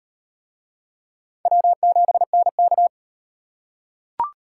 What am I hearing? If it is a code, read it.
Morse code, heard as a keyed tone, about 38 wpm, 700 Hz. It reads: W7NK